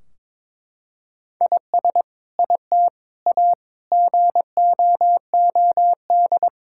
Morse code, heard as a keyed tone, about 22 wpm, 700 Hz. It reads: IS IT A GOOD
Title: Sentences from Top 1600 Words 22wpm